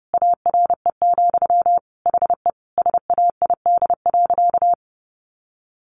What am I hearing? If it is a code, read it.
Morse code, heard as a keyed tone, about 30 wpm, 700 Hz. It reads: ARE, HE SAID.
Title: Sentences from Top 1100 Words 30wpm